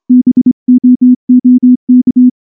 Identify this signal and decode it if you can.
Morse code, heard as a keyed tone, about 25 wpm, 250 Hz. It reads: BOOK
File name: morse / morse-training-25WPM